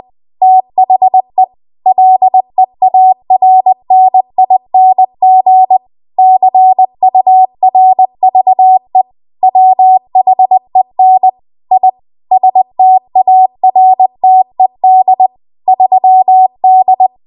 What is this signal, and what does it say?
Morse code, heard as a keyed tone, about 20 wpm, 750 Hz. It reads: THE LEARNING CURVE WHEN I STARTED 3D
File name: Archive / 20 WPM / 20WPM